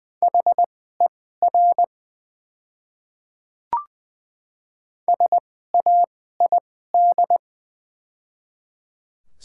Morse code, heard as a keyed tone, about 20 wpm, spaced as for 15 wpm, 700 Hz. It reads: HER SAID
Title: 2x Character Spacing - Top 100 Words - 20wpm